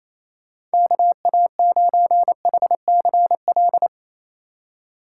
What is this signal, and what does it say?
Morse code, heard as a keyed tone, about 36 wpm, 700 Hz. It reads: KA9HCL